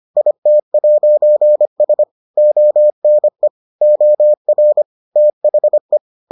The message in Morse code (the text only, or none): IT'S ONE OR THE